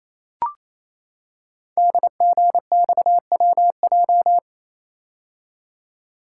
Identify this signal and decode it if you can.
Morse code, heard as a keyed tone, about 27 wpm, 700 Hz. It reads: DGXWJ
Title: Code Groups - 5 Letters - 28wpm